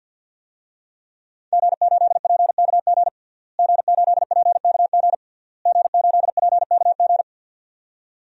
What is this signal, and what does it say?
Morse code, heard as a keyed tone, about 48 wpm, 700 Hz. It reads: G8PKG G8PKG G8PKG
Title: DXCC-100 Call Signs - 5 Characters - Repeated 3x - 50wpm